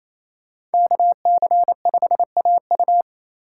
Morse code, heard as a keyed tone, about 28 wpm, 700 Hz. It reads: KC5AU
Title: USA Call Signs - 3x Character Spacing - 28wpm